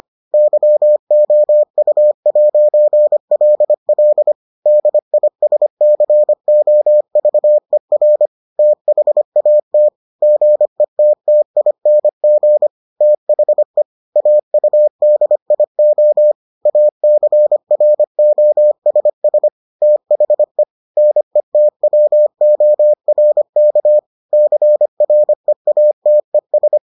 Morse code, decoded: YOU'LL DISCOVER THAT GETTING THE AUDIO ACROSS THE NETWORK CREATES

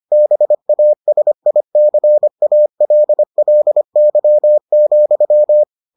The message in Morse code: BASICALLY,

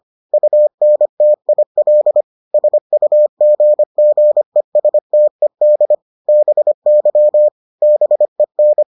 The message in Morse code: UNTIL SUGGESTED BY BEN

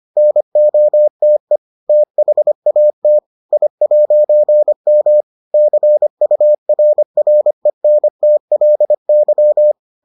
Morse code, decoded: NOTE THAT I'M CURRENTLY

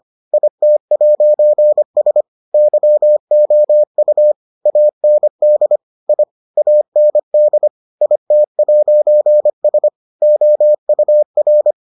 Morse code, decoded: IT'S YOU AND I AND IT'S OUR

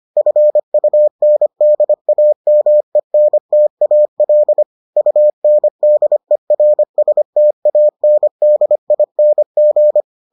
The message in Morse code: FUNDAMENTAL UNDERSTANDING

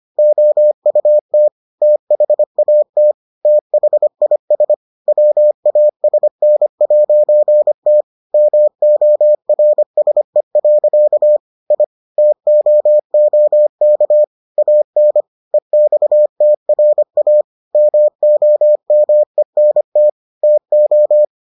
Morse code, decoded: OUT THAT THIS WASN'T MORSE. I TOOK AN EXTRA MOMENT TO

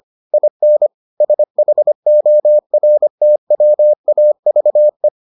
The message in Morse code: IN SHORTWAVE

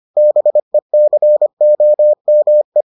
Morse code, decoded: BECOME

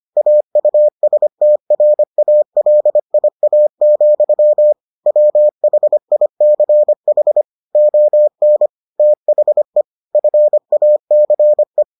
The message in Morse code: AUSTRALIA, WHICH ON THE FACE

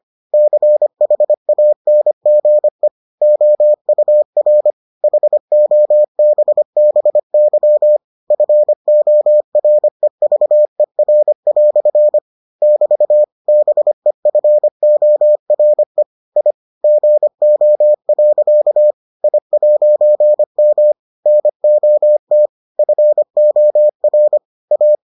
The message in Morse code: CHANGE OUR HOBBY FOREVER" <BT> BEFORE I GO. I'M NOT FOR A

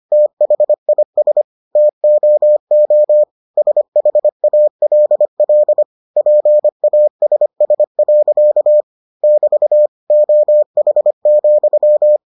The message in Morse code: THIS TOO SHALL PASS. <BT> OH,

